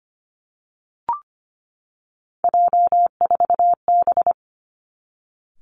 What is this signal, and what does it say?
Morse code, heard as a keyed tone, about 25 wpm, 700 Hz. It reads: J4B
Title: Three Challenging Letter-Number Combinations 25wpm